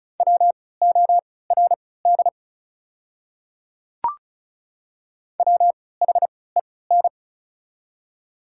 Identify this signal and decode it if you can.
Morse code, heard as a keyed tone, about 34 wpm, spaced as for 20 wpm, 690 Hz. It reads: WORD WHEN
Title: 3x Character Spacing - Top 100 Words - 35wpm